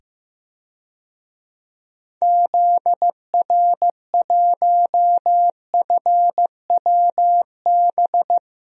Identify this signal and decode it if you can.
Morse code, heard as a keyed tone, about 15 wpm, 700 Hz. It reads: ZR1FWB
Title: DXCC-100 Call Signs - 6 Characters - 15wpm